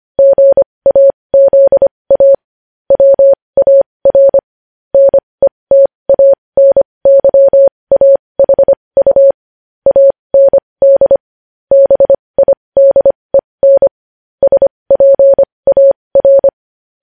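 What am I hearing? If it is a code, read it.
Morse code, heard as a keyed tone, about 25 wpm, spaced as for 20 wpm, 550 Hz. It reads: GAZA WAR NETANYAHU AND BIDEN SPAR
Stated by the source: Morse code news headlines sent at 20 words per minute.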